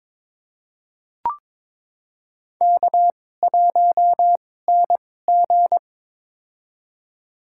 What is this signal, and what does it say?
Morse code, heard as a keyed tone, about 22 wpm, spaced as for 16 wpm, 700 Hz. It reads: K1NG
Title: USA Call Signs - 4 Characters - 2x Character Spacing - 22wpm